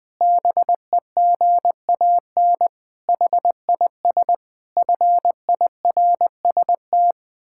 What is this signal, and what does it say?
Morse code, heard as a keyed tone, about 20 wpm, 700 Hz. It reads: BEGAN HIS FIRST
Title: Pangram Challenge v2 - 20wpm